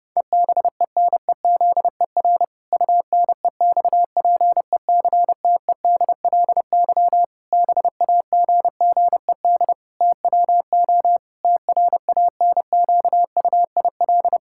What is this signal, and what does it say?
Morse code, heard as a keyed tone, about 31 wpm, 700 Hz. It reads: EBENEZER UNEXPECTEDLY BAGGED TWO TRANQUIL
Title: Pangram Challenge v2 - 30wpm